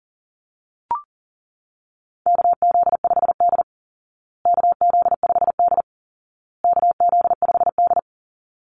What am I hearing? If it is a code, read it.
Morse code, heard as a keyed tone, about 40 wpm, 700 Hz. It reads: KZ5D KZ5D KZ5D
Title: Call Signs Repeated 2x - 40wpm